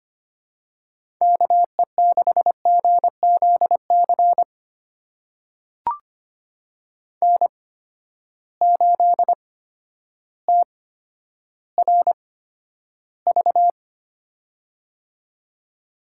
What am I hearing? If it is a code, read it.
Morse code, heard as a keyed tone, about 25 wpm, spaced as for 11 wpm, 700 Hz. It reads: KE6GZC N8TRV
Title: USA Call Signs - 8x Character Spacing - 25wpm